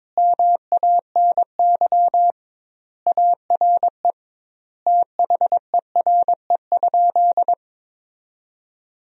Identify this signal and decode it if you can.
Morse code, heard as a keyed tone, about 22 wpm, 700 Hz. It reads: MANY ARE THERE?
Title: 2x Word-Spacing - Sentences from Top 100 Words - 22wpm